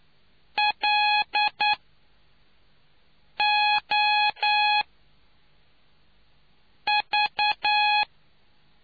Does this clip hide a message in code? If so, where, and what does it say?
Yes: Morse code from 0.6 to 8.0 s, heard as a keyed tone, about 9 wpm, 840 Hz. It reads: L O V